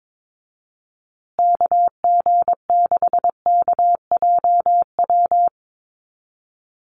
Morse code, heard as a keyed tone, about 22 wpm, 700 Hz. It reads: KG6KJW
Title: USA Call Signs Repeated 2x 22wpm